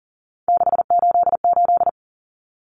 Morse code, heard as a keyed tone, about 40 wpm, 700 Hz. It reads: 688
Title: Numbers - 3-Digit 40wpm